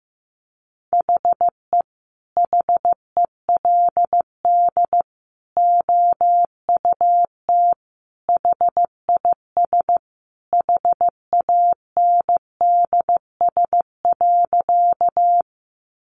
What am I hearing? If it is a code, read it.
Morse code, heard as a keyed tone, about 15 wpm, 700 Hz. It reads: HE HELD OUT HIS HANDS.
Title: A Sound Of Thunder 15wpm